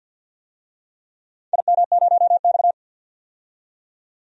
Morse code, heard as a keyed tone, about 52 wpm, 700 Hz. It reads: IM0X
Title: DXCC-100 Call Signs - 4 Characters - 50wpm